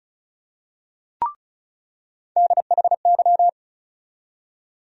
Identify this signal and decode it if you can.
Morse code, heard as a keyed tone, about 34 wpm, 700 Hz. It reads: DHY
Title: USA Call Sign Postfixes - 3 Letters 35wpm